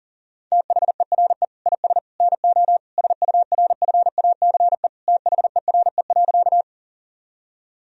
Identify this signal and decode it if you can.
Morse code, heard as a keyed tone, about 40 wpm, 700 Hz. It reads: THERE IS NO SURFACE THERE.
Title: Sentences from Top 1000 Words - Review Part 3 - 40wpm